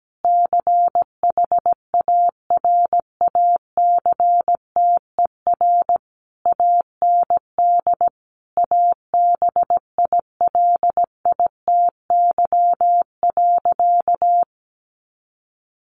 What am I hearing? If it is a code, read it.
Morse code, heard as a keyed tone, about 17 wpm, 700 Hz. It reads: CHARACTER AND ABILITY.
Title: Sentences from Top 1200 Words 17wpm